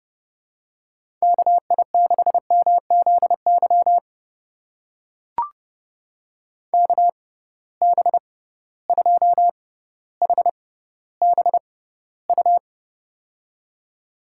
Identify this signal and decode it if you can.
Morse code, heard as a keyed tone, about 30 wpm, spaced as for 15 wpm, 700 Hz. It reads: KI6MZY KB2HBU
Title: USA Call Signs - 6 Characters - 6x Character Spacing - 30wpm